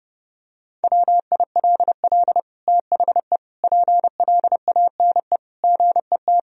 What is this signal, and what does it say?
Morse code, heard as a keyed tone, about 30 wpm, 700 Hz. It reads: WILL THE PLANE GET
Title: Sentences from Top 1000 Words - Review Part 3 - 30wpm